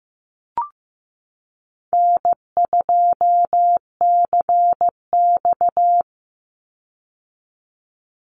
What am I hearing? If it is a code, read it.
Morse code, heard as a keyed tone, about 15 wpm, 700 Hz. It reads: N2CX
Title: Call Signs Repeated 3x - 15wpm